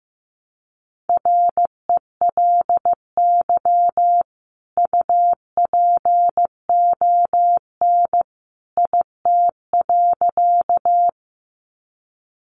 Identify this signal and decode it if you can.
Morse code, heard as a keyed tone, about 15 wpm, 700 Hz. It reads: RELY UPON IT.
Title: Sentences from Top 1200 Words 15wpm